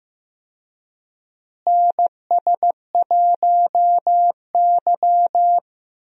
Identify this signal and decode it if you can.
Morse code, heard as a keyed tone, about 15 wpm, 700 Hz. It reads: NS1Y
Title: USA Call Signs - 4 Characters - Repeated 3x - 15wpm